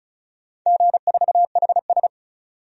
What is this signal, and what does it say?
Morse code, heard as a keyed tone, about 35 wpm, 690 Hz. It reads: G4HS
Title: DXCC-100 Call Signs - 4 Characters - 35wpm